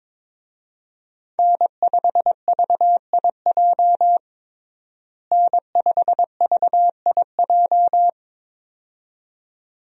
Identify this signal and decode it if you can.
Morse code, heard as a keyed tone, about 22 wpm, 700 Hz. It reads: N5VIJ N5VIJ
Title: USA Call Signs - 5 Characters - Repeated 2x - 22wpm